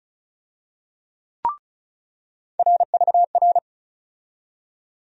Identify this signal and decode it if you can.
Morse code, heard as a keyed tone, about 36 wpm, 700 Hz. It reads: RVR
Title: USA Call Sign Postfixes - 3 Letters 35wpm